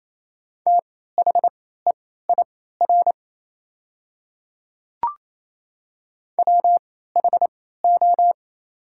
Morse code, heard as a keyed tone, about 27 wpm, spaced as for 16 wpm, 690 Hz. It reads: THEIR WHO
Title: 3x Character Spacing - Top 100 Words - 28wpm